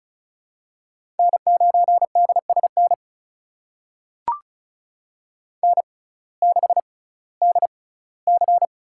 Morse code, heard as a keyed tone, about 36 wpm, spaced as for 19 wpm, 700 Hz. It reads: N9DSN N6DC
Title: USA Call Signs - 5 Characters - 6x Character Spacing - 35wpm